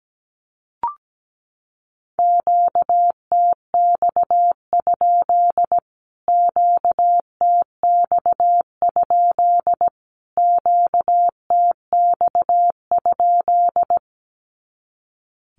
Morse code, heard as a keyed tone, about 17 wpm, 700 Hz. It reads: QTX? QTX? QTX?
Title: Q-Codes Repeated 3x 17wpm